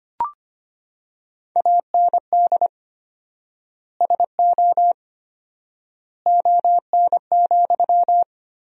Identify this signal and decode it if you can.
Morse code, heard as a keyed tone, about 25 wpm, 700 Hz. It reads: AND SO ON,
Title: 4x Word-Spacing - Sentences from Top 100 Words - 25wpm